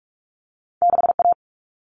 Morse code, heard as a keyed tone, about 46 wpm, 700 Hz. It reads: 6A
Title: All Cut Numbers - 2-Digit 45wpm